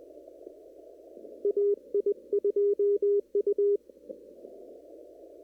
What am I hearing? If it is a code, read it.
Morse code, heard as a keyed tone, about 22 wpm, 410 Hz. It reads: AI2U
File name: CW / kt5j wpx cw-op-k5pi